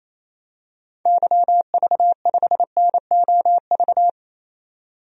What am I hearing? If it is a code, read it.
Morse code, heard as a keyed tone, about 27 wpm, 700 Hz. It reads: YV5NOV
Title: DXCC-100 Call Signs - 6 Characters - 28wpm